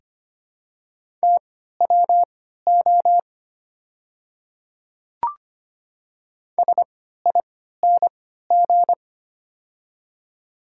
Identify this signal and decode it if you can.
Morse code, heard as a keyed tone, about 25 wpm, spaced as for 14 wpm, 690 Hz. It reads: TWO SING
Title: 3x Character Spacing - Top 500 Words - 25wpm